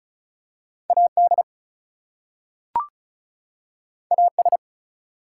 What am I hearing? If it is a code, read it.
Morse code, heard as a keyed tone, about 36 wpm, 700 Hz. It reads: AD AS
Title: Two Letter Words - 35wpm